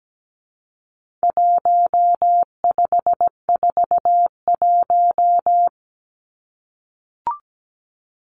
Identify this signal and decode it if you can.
Morse code, heard as a keyed tone, about 17 wpm, 700 Hz. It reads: 1541
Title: Numbers - 4-Digit 17wpm